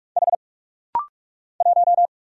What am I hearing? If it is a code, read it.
Morse code, heard as a keyed tone, about 46 wpm, 700 Hz. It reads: H 1
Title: Single Letter-Number - Rapid-Fire 45wpm